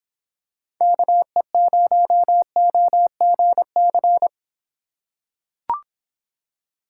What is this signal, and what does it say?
Morse code, heard as a keyed tone, about 23 wpm, 700 Hz. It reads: KE0OGC